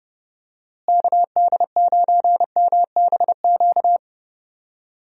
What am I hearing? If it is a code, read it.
Morse code, heard as a keyed tone, about 30 wpm, 700 Hz. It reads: KD9MBQ
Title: USA Call Signs - 6 Characters - Repeated 2x - 30wpm